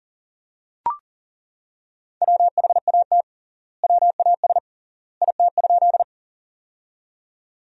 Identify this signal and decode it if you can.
Morse code, heard as a keyed tone, about 40 wpm, 700 Hz. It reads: WHAT WAS IT?
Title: 3x Word-Spacing - Sentences from Top 100 Words - 40wpm